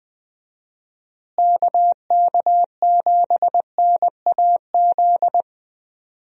Morse code, heard as a keyed tone, about 20 wpm, 700 Hz. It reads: KK7NAZ
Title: USA Call Signs - 6 Characters - Repeated 3x - 20wpm